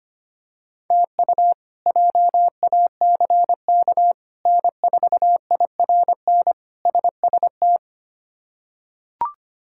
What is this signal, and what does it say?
Morse code, heard as a keyed tone, about 25 wpm, 700 Hz. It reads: TU JACK N4IRN SST
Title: K1USN SST QSO Practice - 25wpm